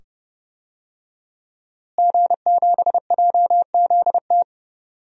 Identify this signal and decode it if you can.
Morse code, heard as a keyed tone, about 31 wpm, 700 Hz. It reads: G7JZT
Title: DXCC-100 Call Signs - 5 Characters - Repeated 2x - 30wpm